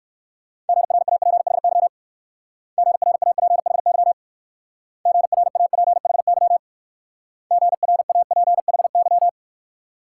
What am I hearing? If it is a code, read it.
Morse code, heard as a keyed tone, about 59 wpm, 690 Hz. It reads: GRAPHY GRAPHY GRAPHY GRAPHY